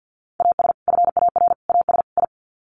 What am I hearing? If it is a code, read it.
Morse code, heard as a keyed tone, about 50 wpm, 700 Hz. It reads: AS FAR AS I
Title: Sets of 4 Words 50wpm